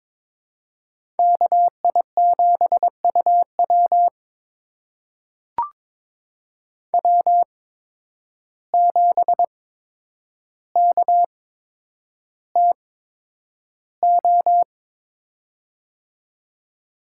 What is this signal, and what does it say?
Morse code, heard as a keyed tone, about 22 wpm, spaced as for 9 wpm, 700 Hz. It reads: KI7UW W7KTO
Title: USA Call Signs - 5 Characters - 8x Character Spacing - 22wpm